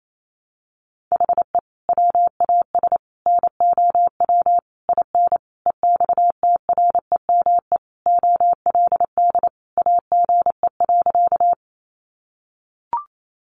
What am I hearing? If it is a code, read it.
Morse code, heard as a keyed tone, about 28 wpm, 700 Hz. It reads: HE WAS NOW IN EXTREME OLD AGE.
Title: Sentences from Top 1300 Words 28wpm